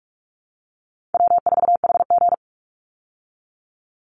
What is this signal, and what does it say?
Morse code, heard as a keyed tone, about 44 wpm, 700 Hz. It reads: W4HG
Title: Easy Call Signs Repeated 2x - 45wpm